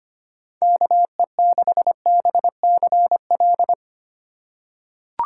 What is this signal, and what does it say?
Morse code, heard as a keyed tone, about 25 wpm, 700 Hz. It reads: KE6BCL
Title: USA Call Signs - 3x Character Spacing - 25wpm